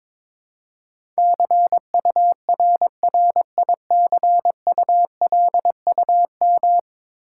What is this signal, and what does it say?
Morse code, heard as a keyed tone, about 23 wpm, 700 Hz. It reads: CURRICULUM
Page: morse-code-ninja-practice-22wpm